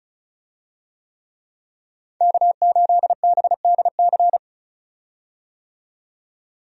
Morse code, heard as a keyed tone, about 36 wpm, 700 Hz. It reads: K8BDC
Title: USA Call Signs - 5 Characters - 35wpm